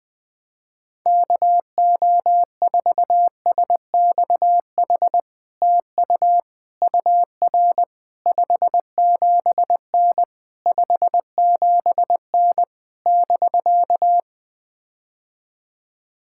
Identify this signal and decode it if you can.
Morse code, heard as a keyed tone, about 20 wpm, 700 Hz. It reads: KO4SXH TU UR 57N 57N <BK>
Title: POTA QSO Practice - 20wpm